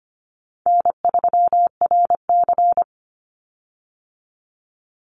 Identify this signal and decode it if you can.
Morse code, heard as a keyed tone, about 25 wpm, 700 Hz. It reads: N3RC
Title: Call Signs Repeated 2x - 25wpm